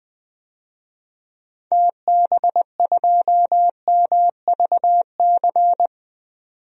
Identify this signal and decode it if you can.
Morse code, heard as a keyed tone, about 20 wpm, 700 Hz. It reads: TB2MVC
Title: DXCC-100 Call Signs - 6 Characters - 20wpm